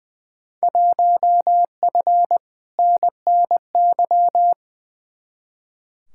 Morse code, heard as a keyed tone, about 20 wpm, 700 Hz. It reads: 1F NNY
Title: Field Day Exchanges - 20wpm